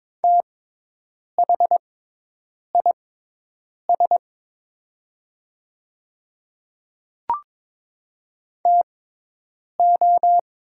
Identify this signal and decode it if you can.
Morse code, heard as a keyed tone, about 22 wpm, spaced as for 8 wpm, 690 Hz. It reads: THIS TO
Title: 6x Character Spacing - Top 100 Words - 22wpm